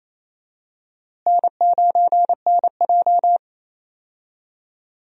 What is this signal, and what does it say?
Morse code, heard as a keyed tone, about 27 wpm, 700 Hz. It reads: N9NJ
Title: USA Call Signs - 4 Characters - Repeated 2x - 28wpm